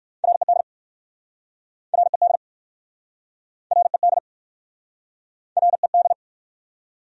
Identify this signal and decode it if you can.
Morse code, heard as a keyed tone, about 58 wpm, 700 Hz. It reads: RED RED RED RED